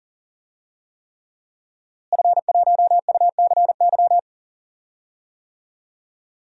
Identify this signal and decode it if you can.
Morse code, heard as a keyed tone, about 37 wpm, 700 Hz. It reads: F1UCY
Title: DXCC-100 Call Signs - 5 Characters - Repeated 3x - 40wpm